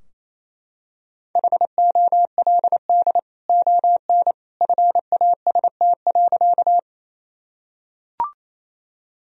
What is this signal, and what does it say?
Morse code, heard as a keyed tone, about 28 wpm, 700 Hz. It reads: HOLD ON FAST.
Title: Sentences from Top 1000 Words - Review Part 2 - 28wpm